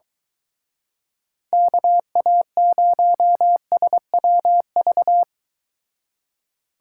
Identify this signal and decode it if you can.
Morse code, heard as a keyed tone, about 23 wpm, 700 Hz. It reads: KA0SWV